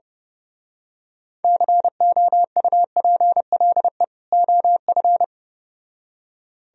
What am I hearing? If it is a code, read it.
Morse code, heard as a keyed tone, about 29 wpm, 700 Hz. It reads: COUPLE OF
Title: Speed-Racing - Sets of 2 Words 25wpm